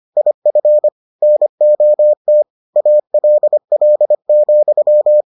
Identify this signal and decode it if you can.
Morse code, heard as a keyed tone, about 25 wpm, 600 Hz. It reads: IF NOT ALL,